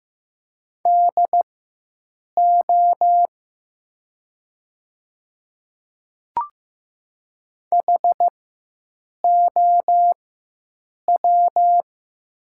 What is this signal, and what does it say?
Morse code, heard as a keyed tone, about 15 wpm, spaced as for 7 wpm, 700 Hz. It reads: DO HOW
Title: 4x Character Spacing - Top 100 Words - 15wpm